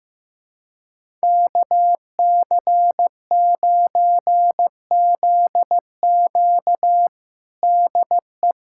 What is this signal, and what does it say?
Morse code, heard as a keyed tone, about 15 wpm, 700 Hz. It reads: KC9ZQ DE
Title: USA Callsign DE Callsign - 5 Characters - 15wpm